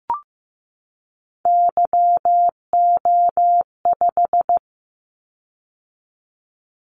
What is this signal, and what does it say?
Morse code, heard as a keyed tone, about 15 wpm, 700 Hz. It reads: YO5
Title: Callsign Prefixes for Top 25 DXCC - 15wpm